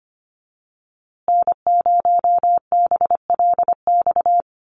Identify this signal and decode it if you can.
Morse code, heard as a keyed tone, about 25 wpm, 700 Hz. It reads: N0BLX
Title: USA Call Signs Repeated 2x 25wpm